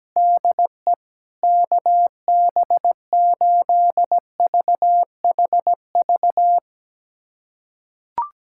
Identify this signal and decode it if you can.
Morse code, heard as a keyed tone, about 17 wpm, 700 Hz. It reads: DE KB8VHV
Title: USA Callsign DE Callsign - 6 Characters - 17wpm